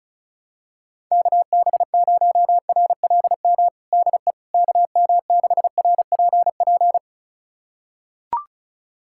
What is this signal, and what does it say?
Morse code, heard as a keyed tone, about 34 wpm, 700 Hz. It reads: KB0RLM DE KM6RPP
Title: USA Callsign DE Callsign - 6 Characters - 35wpm